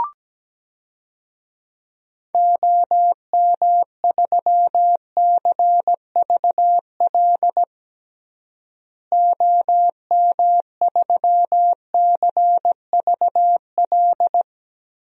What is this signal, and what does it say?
Morse code, heard as a keyed tone, about 17 wpm, 700 Hz. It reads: OM3CVL OM3CVL
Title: DXCC-100 Call Signs - 6 Characters - Repeated 3x - 17wpm